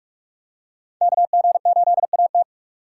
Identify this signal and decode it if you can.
Morse code, heard as a keyed tone, about 44 wpm, 700 Hz. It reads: KG8AT